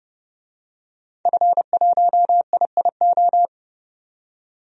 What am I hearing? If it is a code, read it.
Morse code, heard as a keyed tone, about 29 wpm, 700 Hz. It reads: F1IIO
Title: DXCC-100 Call Signs - 5 Characters - 30wpm